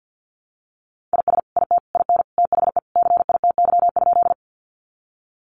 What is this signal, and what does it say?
Morse code, heard as a keyed tone, about 50 wpm, 700 Hz. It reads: IS IT IN THE CITY?
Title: Sentences from Top 300 Words 50wpm